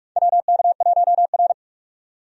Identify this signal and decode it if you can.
Morse code, heard as a keyed tone, about 46 wpm, 700 Hz. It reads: WK1R
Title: USA Call Signs - 4 Characters - 45wpm